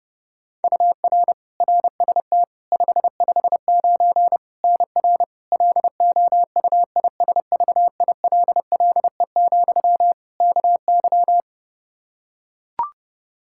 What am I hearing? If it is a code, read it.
Morse code, heard as a keyed tone, about 30 wpm, 700 Hz. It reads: UR RST 559 NR LOUISVILLE, KY
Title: K1USN SST QSO Practice 30wpm